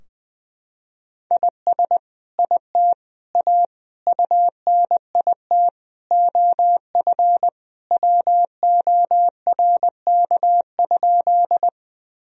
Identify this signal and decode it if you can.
Morse code, heard as a keyed tone, about 20 wpm, 700 Hz. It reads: IS IT A UNIT OF WORK?
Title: Sentences from Top 1000 Words - Review Part 2 - 20wpm